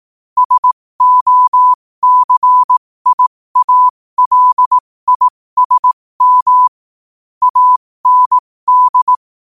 Morse code, heard as a keyed tone, about 18 wpm, spaced as for 15 wpm, 1000 Hz. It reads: SOCIALISM AND
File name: qotdfiles / 15wpm